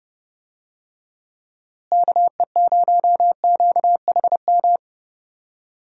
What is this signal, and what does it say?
Morse code, heard as a keyed tone, about 30 wpm, 700 Hz. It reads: KE0QHM
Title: USA Call Signs - 6 Characters - Repeated 2x - 30wpm